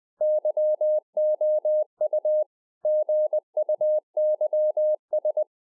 Morse code, heard as a keyed tone, about 20 wpm, 610 Hz. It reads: YOU GUYS